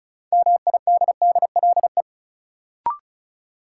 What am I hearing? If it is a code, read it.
Morse code, heard as a keyed tone, about 36 wpm, 700 Hz. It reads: MIDDLE
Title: Sentences from Top 900 Words - New Word Drill - 35wpm